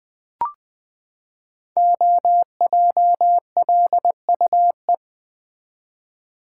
Morse code, heard as a keyed tone, about 20 wpm, 700 Hz. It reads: OJLUE
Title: Code Groups - 5 Letter:Numbers - 20wpm